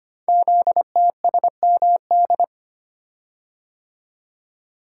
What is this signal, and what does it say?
Morse code, heard as a keyed tone, about 25 wpm, 700 Hz. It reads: ZTSMD
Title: Code Groups - 5 Letters - 25wpm